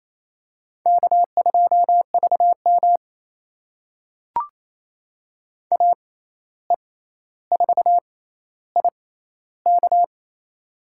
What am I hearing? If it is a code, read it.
Morse code, heard as a keyed tone, about 28 wpm, spaced as for 13 wpm, 700 Hz. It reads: K2VM AE4IK
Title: USA Call Signs - 6x Character Spacing - 28wpm